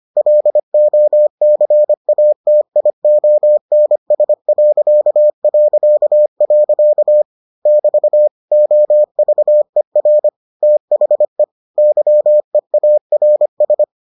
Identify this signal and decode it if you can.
Morse code, heard as a keyed tone, about 25 wpm, 600 Hz. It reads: LOCATIONS... <BT> OVER THE YEARS